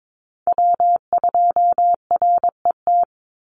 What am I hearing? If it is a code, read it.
Morse code, heard as a keyed tone, about 22 wpm, 700 Hz. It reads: W2RET
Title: USA Call Signs 22wpm